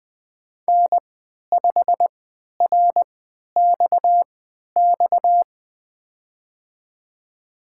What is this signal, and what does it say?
Morse code, heard as a keyed tone, about 20 wpm, spaced as for 11 wpm, 700 Hz. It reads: N5RXX
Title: USA Call Signs - 5 Characters - 3x Character Spacing - 20wpm